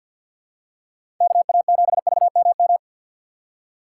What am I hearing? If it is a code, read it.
Morse code, heard as a keyed tone, about 49 wpm, 700 Hz. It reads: KA7VMM
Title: USA Call Signs - 6 Characters - 6x Character Spacing - 50wpm